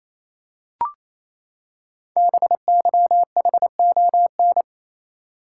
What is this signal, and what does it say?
Morse code, heard as a keyed tone, about 27 wpm, 700 Hz. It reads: BYHON
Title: Code Groups - 5 Letters - 28wpm